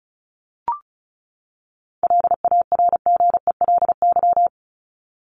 Morse code, heard as a keyed tone, about 34 wpm, 700 Hz. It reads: LARGELY
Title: LY Suffix - Top 100 Words with Base Words 35wpm